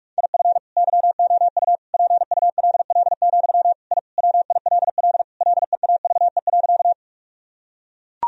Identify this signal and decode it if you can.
Morse code, heard as a keyed tone, about 41 wpm, 700 Hz. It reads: IF YOU PULL, I WILL LEAVE.
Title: Sentences from Top 1000 Words - Review Part 2 - 45wpm